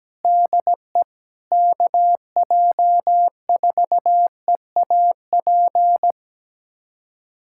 Morse code, heard as a keyed tone, about 17 wpm, 700 Hz. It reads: DE KJ4EAP
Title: USA Callsign DE Callsign 17wpm